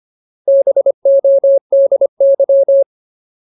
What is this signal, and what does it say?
Morse code, heard as a keyed tone, about 25 wpm, 550 Hz. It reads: BODY
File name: Ultimate Word List / 25WPM Ultimate